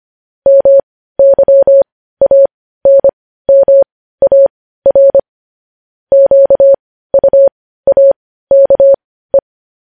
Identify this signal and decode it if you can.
Morse code, heard as a keyed tone, about 25 wpm, spaced as for 15 wpm, 550 Hz. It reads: MYANMAR QUAKE
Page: News Headlines in Morse Code at 15 WPM